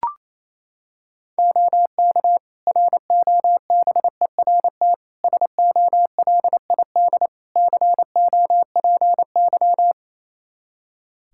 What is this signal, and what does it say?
Morse code, heard as a keyed tone, about 28 wpm, 700 Hz. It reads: OK ROBERT SOLID COPY
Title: K1USN SST QSO Practice 28wpm